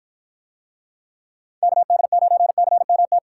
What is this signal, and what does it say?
Morse code, heard as a keyed tone, about 53 wpm, 700 Hz. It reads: KD9CMT